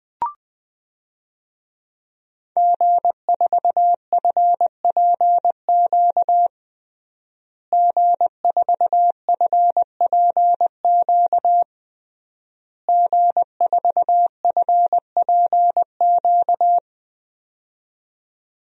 Morse code, heard as a keyed tone, about 20 wpm, 700 Hz. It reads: G4FPQ G4FPQ G4FPQ
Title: DXCC-100 Call Signs - 5 Characters - Repeated 3x - 20wpm